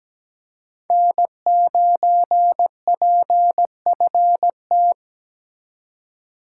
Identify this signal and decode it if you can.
Morse code, heard as a keyed tone, about 17 wpm, 700 Hz. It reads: N9PFT
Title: USA Call Signs - 5 Characters - 2x Character Spacing - 17wpm